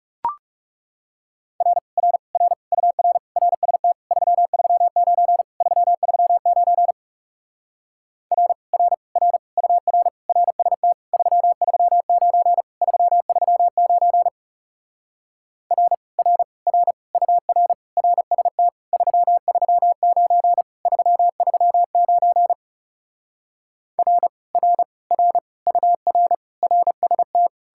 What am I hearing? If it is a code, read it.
Morse code, heard as a keyed tone, about 37 wpm, 700 Hz. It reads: R R R UR RST 339 339 R R R UR RST 339 339 R R R UR RST 339 339 R R R UR RST
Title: Speed-Racing Common QSO Phrases Encore - 30wpm